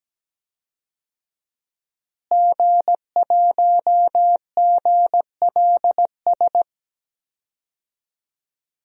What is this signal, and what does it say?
Morse code, heard as a keyed tone, about 17 wpm, 700 Hz. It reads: G1GLS
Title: DXCC-100 Call Signs - 5 Characters - Repeated 2x - 17wpm